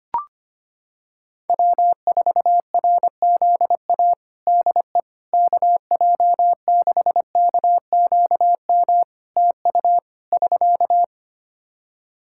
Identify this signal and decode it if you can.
Morse code, heard as a keyed tone, about 25 wpm, 700 Hz. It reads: W4RZA DE KJ6KQM TU <SK>
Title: Practice QSO - Exchange 4 - Ending - 25wpm